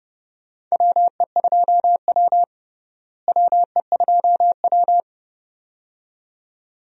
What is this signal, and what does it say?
Morse code, heard as a keyed tone, about 30 wpm, 700 Hz. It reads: WE2W WE2W
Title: USA Call Signs - 4 Characters - Repeated 2x - 30wpm